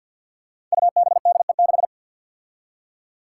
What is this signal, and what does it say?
Morse code, heard as a keyed tone, about 53 wpm, 700 Hz. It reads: UBDE6
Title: Five Challenging Letter-Number Combinations 50wpm